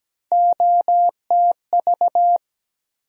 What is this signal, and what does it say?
Morse code, heard as a keyed tone, about 17 wpm, 700 Hz. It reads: OTV
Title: USA Call Sign Postfixes - 3 Letters 17wpm